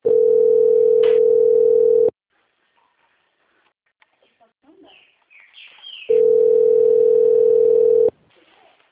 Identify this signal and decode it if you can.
Ringback tone (North America), 440 + 480 Hz, with speech.
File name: out- - - - - .13510